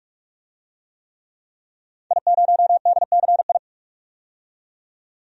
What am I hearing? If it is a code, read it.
Morse code, heard as a keyed tone, about 46 wpm, 700 Hz. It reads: I0DCI
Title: DXCC-100 Call Signs - 5 Characters - 45wpm